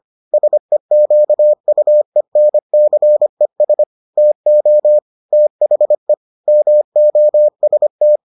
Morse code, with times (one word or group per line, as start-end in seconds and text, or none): 0.3-3.8 SEQUENCES
4.2-5.0 TO
5.3-6.1 THE
6.5-8.2 MOST